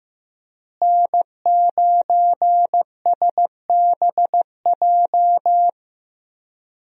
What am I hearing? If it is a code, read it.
Morse code, heard as a keyed tone, about 15 wpm, 700 Hz. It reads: N9SBJ
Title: USA Call Signs - 5 Characters - Repeated 3x - 15wpm